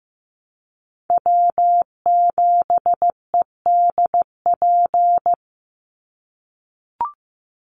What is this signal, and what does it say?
Morse code, heard as a keyed tone, about 15 wpm, 700 Hz. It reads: W7EDP
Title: USA Call Signs 15wpm